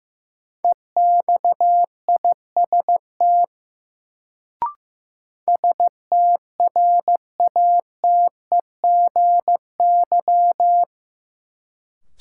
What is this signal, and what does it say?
Morse code, heard as a keyed tone, about 16 wpm, 700 Hz. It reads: EXIST STRATEGY
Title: Sentences from Top 1000 Words - Review Part 5 - New Word Drill - 15wpm